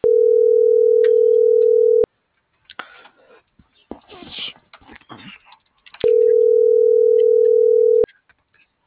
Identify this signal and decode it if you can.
Ringback tone (North America), 440 + 480 Hz.
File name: CENTRAL MULTIMODAL / out angel.velazquez